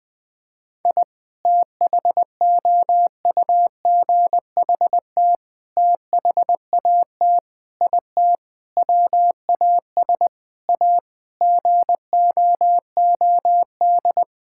Morse code, heard as a keyed tone, about 20 wpm, 700 Hz. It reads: I THOUGHT THAT IT WAS A GOOD
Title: Sentences with 4-Word Phrases - 20wpm